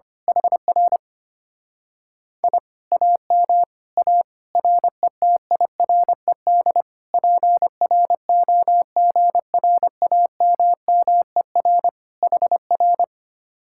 Morse code, heard as a keyed tone, about 24 wpm, 700 Hz. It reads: HR I AM A RETIRED PROGRAMMER HR
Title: Speed-Racing Common QSO Phrases - 25wpm